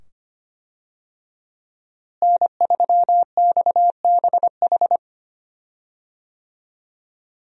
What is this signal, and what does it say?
Morse code, heard as a keyed tone, about 25 wpm, 700 Hz. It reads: N3XBH
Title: USA Call Signs - 5 Characters - Repeated 2x - 25wpm